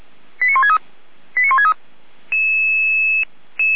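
Five-tone selective call (ZVEI), heard as quick runs of five tones, tones of about 70 ms. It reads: 89163 89163
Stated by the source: ZVEI five-tone call 89123 received by SDR, logged by feuerwehr-orlamuende.org fire brigade server.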